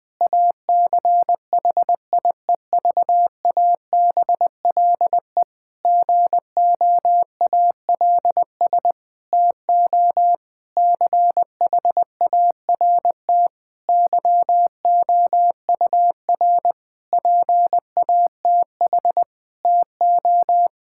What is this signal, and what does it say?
Morse code, heard as a keyed tone, about 20 wpm, 700 Hz. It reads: ACHIEVABLE GOALS TO CHART YOUR PATH TO
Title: Difficult Sentences with Hints - 20wpm